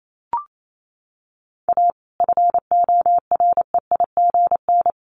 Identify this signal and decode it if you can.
Morse code, heard as a keyed tone, about 28 wpm, 700 Hz. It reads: A FOREIGN
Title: Sentences from Top 1400 Words 28wpm